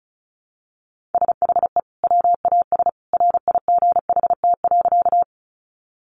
Morse code, heard as a keyed tone, about 34 wpm, 700 Hz. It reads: SHE WAS RIGHT.
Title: Sentences from Top 200 Words 35wpm